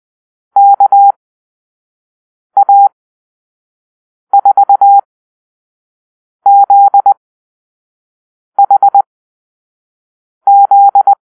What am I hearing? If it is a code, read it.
Morse code, heard as a keyed tone, about 20 wpm, spaced as for 5 wpm, 800 Hz. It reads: KA4ZHZ
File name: arrl / Call-Sign Drill-4